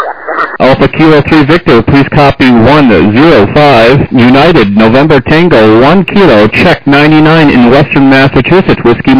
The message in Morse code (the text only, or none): none